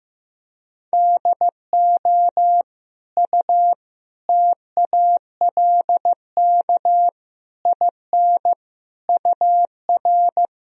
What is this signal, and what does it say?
Morse code, heard as a keyed tone, about 15 wpm, 700 Hz. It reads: DO U TALK IN UR
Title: Sentences from Top 500 Words and Abbreviations 15wpm